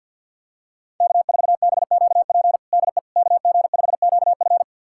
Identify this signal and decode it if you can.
Morse code, heard as a keyed tone, about 51 wpm, 700 Hz. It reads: K4BQP DE KG5QF
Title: USA Callsign DE Callsign 50wpm